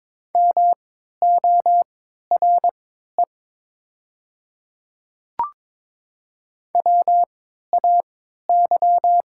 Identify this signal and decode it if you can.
Morse code, heard as a keyed tone, about 22 wpm, spaced as for 13 wpm, 700 Hz. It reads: MORE WAY
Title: 3x Character Spacing - Top 200 Words - 22wpm